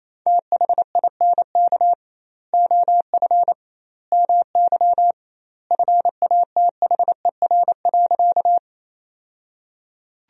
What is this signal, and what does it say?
Morse code, heard as a keyed tone, about 28 wpm, 700 Hz. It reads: THINK OF MY FATHER.
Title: 2x Word-Spacing - Sentences from Top 300 Words - 28wpm